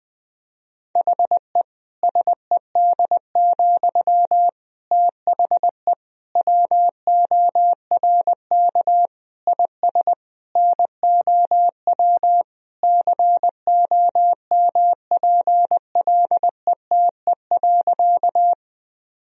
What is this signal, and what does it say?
Morse code, heard as a keyed tone, about 20 wpm, 700 Hz. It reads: HE SED, THE WORK IS NOW COMPLETE.
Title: Sentences from Top 400 Words and Abbreviations 20wpm